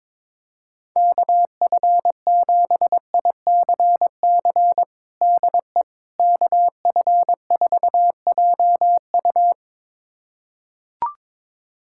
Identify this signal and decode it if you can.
Morse code, heard as a keyed tone, about 22 wpm, 700 Hz. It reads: KF7ICC DE KF4JU
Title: USA Callsign DE Callsign 22wpm